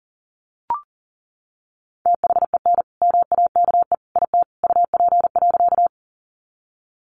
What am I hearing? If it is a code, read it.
Morse code, heard as a keyed tone, about 40 wpm, 700 Hz. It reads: THEN MAKE IT UP.
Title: Sentences from Top 100 Words 40wpm